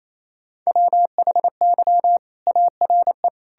Morse code, heard as a keyed tone, about 28 wpm, 700 Hz. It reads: WHY ARE
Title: Sentences from Top 1000 Words - Review Part 6 - 28wpm